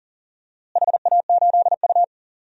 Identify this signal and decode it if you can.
Morse code, heard as a keyed tone, about 40 wpm, 700 Hz. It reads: HA8U
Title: DXCC-100 Call Signs - 4 Characters - Repeated 3x - 40wpm